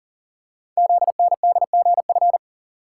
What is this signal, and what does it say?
Morse code, heard as a keyed tone, about 40 wpm, 690 Hz. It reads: ZNDGF
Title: Code Groups - 5 Letters - 40wpm